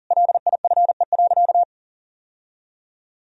Morse code, heard as a keyed tone, about 40 wpm, 700 Hz. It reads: LIFE.
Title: Sentences from Top 1000 Words - Review Part 2 - 40wpm